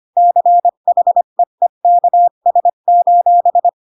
Morse code, heard as a keyed tone, about 25 wpm, 700 Hz. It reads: CHEEKS:
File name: Pinocchio / Pinocchio ch7 20 25 ik1hge